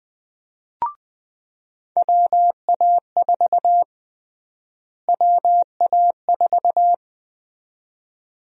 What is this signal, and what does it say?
Morse code, heard as a keyed tone, about 20 wpm, 700 Hz. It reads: WA4 WA4
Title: USA Call Sign Prefixes - Repeated 2x - 20wpm